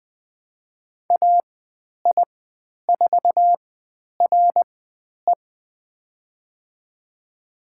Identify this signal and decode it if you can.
Morse code, heard as a keyed tone, about 20 wpm, spaced as for 10 wpm, 700 Hz. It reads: AI4RE
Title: 095 Lesson 16g - Letter c - Call Signs 20wpm